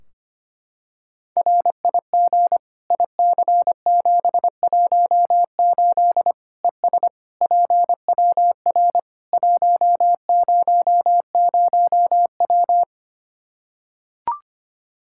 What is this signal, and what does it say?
Morse code, heard as a keyed tone, about 25 wpm, 700 Hz. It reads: RIG IC718 ES PWR 100W
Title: Practice QSO - Exchange 2 - Info - 25wpm